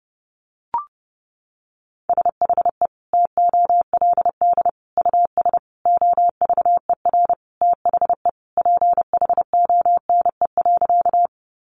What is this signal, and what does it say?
Morse code, heard as a keyed tone, about 30 wpm, 700 Hz. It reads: SHE TOLD US OVER THE PHONE.
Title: Sentences from Top 1100 Words 30wpm